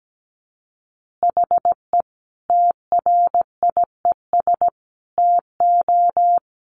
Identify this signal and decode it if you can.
Morse code, heard as a keyed tone, about 17 wpm, 700 Hz. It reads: HE TRIES TO
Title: Additional Easy Phrases 17wpm